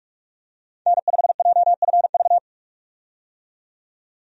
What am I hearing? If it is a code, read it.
Morse code, heard as a keyed tone, about 42 wpm, 700 Hz. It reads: N5JFV
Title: USA Call Signs - 5 Characters - 45wpm